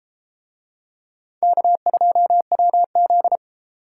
Morse code, heard as a keyed tone, about 28 wpm, 700 Hz. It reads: K2WZ